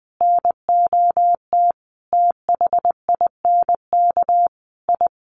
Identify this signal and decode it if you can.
Morse code, heard as a keyed tone, about 20 wpm, 700 Hz. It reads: NOT THINK I
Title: Sets of 5 Words 20wpm